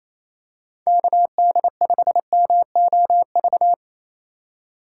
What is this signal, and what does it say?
Morse code, heard as a keyed tone, about 27 wpm, 700 Hz. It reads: KD5MOV
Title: USA Call Signs - 6 Characters - 2x Character Spacing - 28wpm